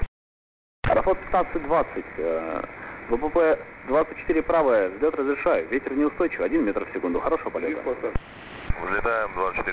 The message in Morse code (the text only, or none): none